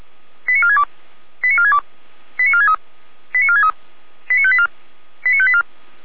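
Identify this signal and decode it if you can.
Five-tone selective call (ZVEI), heard as quick runs of five tones, tones of about 70 ms. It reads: 89462 89462 89463 89463 89584 89584
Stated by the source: ZVEI five-tone selective call 89436 received by SDR.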